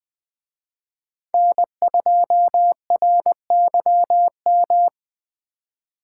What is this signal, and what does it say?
Morse code, heard as a keyed tone, about 24 wpm, 700 Hz. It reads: N2RYM
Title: Speed Racing - USA 1x3 Call Signs 20wpm